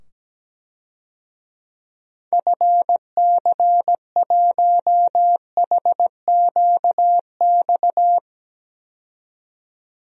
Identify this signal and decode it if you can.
Morse code, heard as a keyed tone, about 17 wpm, 700 Hz. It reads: FC1HQX
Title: DXCC-100 Call Signs - 6 Characters - 17wpm